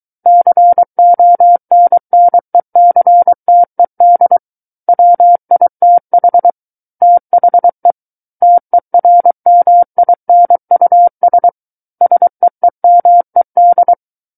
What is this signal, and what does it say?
Morse code, heard as a keyed tone, about 23 wpm, 700 Hz. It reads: CONNECTED WITH THE TERMINUS SEEMED
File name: Book One / 14-Chapter 14 - 23 WPM